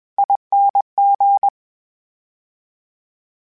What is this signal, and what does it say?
Morse code, heard as a keyed tone, about 20 wpm, 800 Hz. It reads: ING
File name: Special INGs-mp3s / Special INGs-mp3s / VerySlowToFaster Ing @20